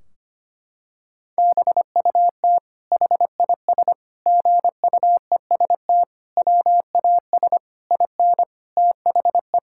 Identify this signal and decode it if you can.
Morse code, heard as a keyed tone, about 25 wpm, 700 Hz. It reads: BUT HIS GUEST WAS IN THE
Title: Sentences from Top 1600 Words 25wpm